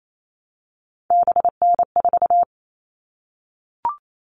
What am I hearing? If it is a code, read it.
Morse code, heard as a keyed tone, about 28 wpm, 700 Hz. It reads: BN4
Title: All Cut Numbers - 3-Digit 28wpm